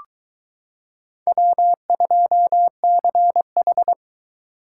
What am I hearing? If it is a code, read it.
Morse code, heard as a keyed tone, about 22 wpm, 700 Hz. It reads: W2CH